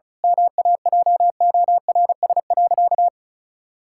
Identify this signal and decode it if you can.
Morse code, heard as a keyed tone, about 36 wpm, 700 Hz. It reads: MAJORS.
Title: Pangram Challenge v2 - 35wpm